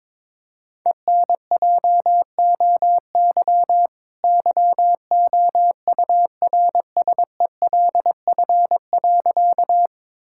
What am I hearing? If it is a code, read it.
Morse code, heard as a keyed tone, about 22 wpm, 700 Hz. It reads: ENJOY YOURSELF.
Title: Sentences from Top 1000 Words - Review Part 5 - 22wpm